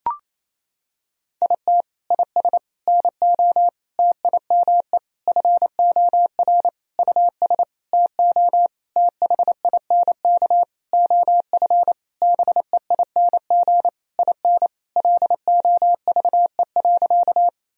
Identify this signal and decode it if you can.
Morse code, heard as a keyed tone, about 28 wpm, 700 Hz. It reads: IT IS NO TIME FOR US TO THINK OF BEING IN LOVE.
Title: Sentences from Top 1000 Words - Review Part 3 - 28wpm